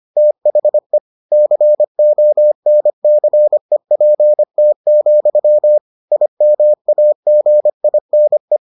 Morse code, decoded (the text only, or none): THE CONCEPT, IMAGINE